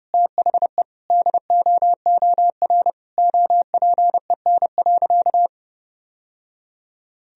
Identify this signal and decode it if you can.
Morse code, heard as a keyed tone, about 30 wpm, 700 Hz. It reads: THE DOOR OPEN.
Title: Sentences from Top 1000 Words - Review Part 1 - 30wpm